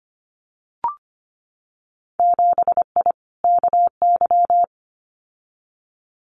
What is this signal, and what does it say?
Morse code, heard as a keyed tone, about 25 wpm, 700 Hz. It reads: 7I KY
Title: Winter Field Day Exchanges Encore 25wpm